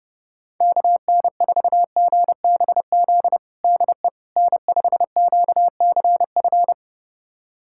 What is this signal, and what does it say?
Morse code, heard as a keyed tone, about 32 wpm, 700 Hz. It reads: KN4GBZ DE N5QCF
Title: USA Callsign DE Callsign 30wpm